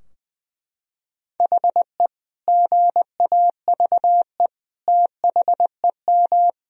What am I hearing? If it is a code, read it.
Morse code, heard as a keyed tone, about 20 wpm, 700 Hz. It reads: HE GAVE THEM
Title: Sentences from Top 1000 Words - Review Part 3 - 20wpm